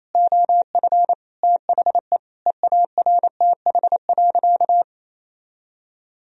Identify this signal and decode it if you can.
Morse code, heard as a keyed tone, about 28 wpm, 700 Hz. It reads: OF THE EARTH.
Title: Sentences from Top 1000 Words - Review Part 3 - 28wpm